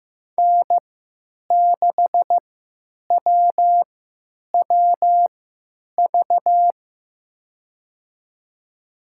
Morse code, heard as a keyed tone, about 16 wpm, spaced as for 9 wpm, 700 Hz. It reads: N6WWV
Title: USA Call Signs - 5 Characters - 3x Character Spacing - 15wpm